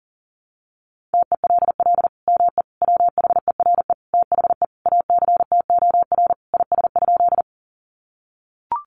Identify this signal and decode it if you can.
Morse code, heard as a keyed tone, about 37 wpm, 700 Hz. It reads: TELL ME WHERE THE ACTOR IS?
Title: Sentences from Top 1500 Words 40wpm